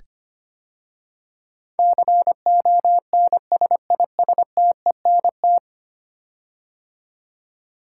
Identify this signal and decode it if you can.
Morse code, heard as a keyed tone, about 25 wpm, 700 Hz. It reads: CONSISTENT
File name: 10 Letter Words - First 3 Letters Given - 25wpm